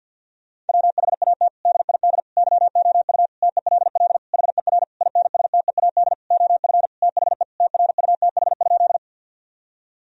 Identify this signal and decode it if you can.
Morse code, heard as a keyed tone, about 52 wpm, 700 Hz. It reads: WHAT DID YOU TELL HER INSTEAD OF THE TRUTH?
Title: Sentences from Top 1000 Words - Review Part 4 - 50wpm